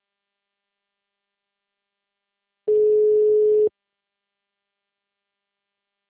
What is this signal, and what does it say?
Ringback tone (Europe, 425 Hz).
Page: 177.136.245.137 Itu